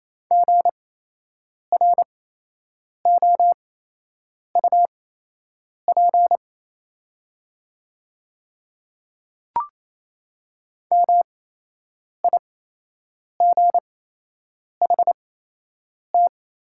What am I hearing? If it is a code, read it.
Morse code, heard as a keyed tone, about 28 wpm, spaced as for 8 wpm, 700 Hz. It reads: GROUP MIGHT
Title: 8x Character Spacing - Top 500 Words - 28wpm